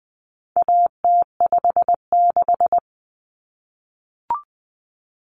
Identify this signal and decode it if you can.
Morse code, heard as a keyed tone, about 20 wpm, 700 Hz. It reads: AT56
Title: Common Cut Numbers - 4-Digit 20wpm